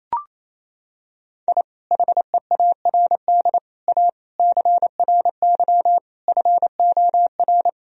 Morse code, heard as a keyed tone, about 28 wpm, 700 Hz. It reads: I HEARD A CRY FOR
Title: Sentences from Top 1000 Words - Review Part 2 - 28wpm